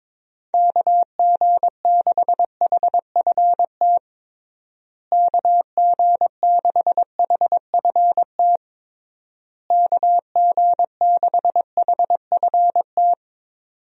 Morse code, heard as a keyed tone, about 22 wpm, 700 Hz. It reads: KG6HFT KG6HFT KG6HFT
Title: USA Call Signs - 6 Characters - Repeated 3x - 22wpm